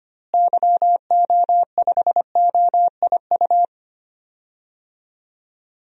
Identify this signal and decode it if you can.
Morse code, heard as a keyed tone, about 25 wpm, 700 Hz. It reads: YO5OIU
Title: DXCC-100 Call Signs - 6 Characters - Repeated 3x - 25wpm